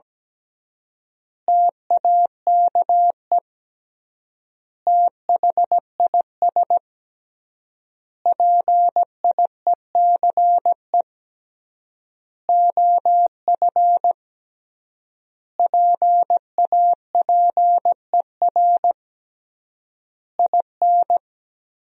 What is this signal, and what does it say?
Morse code, heard as a keyed tone, about 17 wpm, 700 Hz. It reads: TAKE THIS PIECE OF PAPER IN
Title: 3x Word-Spacing - Sentences from Top 400 Words - 17wpm